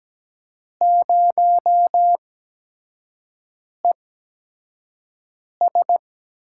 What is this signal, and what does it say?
Morse code, heard as a keyed tone, about 17 wpm, 700 Hz. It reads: 0 E S
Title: USA Call Signs - 5 Characters - 8x Character Spacing - 17wpm